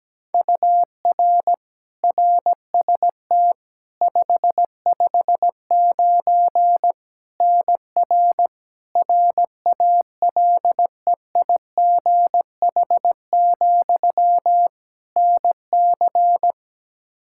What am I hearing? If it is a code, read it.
Morse code, heard as a keyed tone, about 17 wpm, 700 Hz. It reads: UR RST 559 NR RALEIGH, NC
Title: Practice QSO - Exchange 1 - Beginning - 17wpm